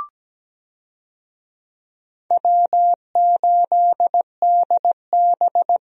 Morse code, heard as a keyed tone, about 17 wpm, 700 Hz. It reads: W8DB
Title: USA Call Signs - 4 Characters - 17wpm